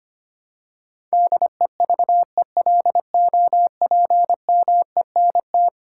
Morse code, heard as a keyed tone, about 25 wpm, 700 Hz. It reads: DEVELOPMENT
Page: morse-code-ninja-practice-25wpm